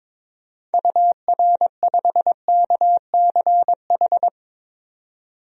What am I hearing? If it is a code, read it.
Morse code, heard as a keyed tone, about 22 wpm, 700 Hz. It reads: UR5KCH
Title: DXCC-100 Call Signs - 6 Characters - 22wpm